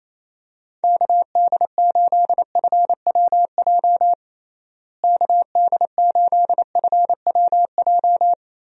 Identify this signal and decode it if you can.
Morse code, heard as a keyed tone, about 28 wpm, 700 Hz. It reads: KD8FWJ KD8FWJ
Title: USA Call Signs - 6 Characters - Repeated 3x - 28wpm